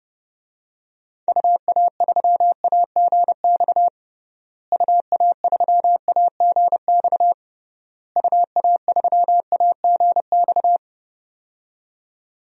Morse code, heard as a keyed tone, about 32 wpm, 700 Hz. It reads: UA3AGX UA3AGX UA3AGX
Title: DXCC-100 Call Signs - 6 Characters - Repeated 3x - 30wpm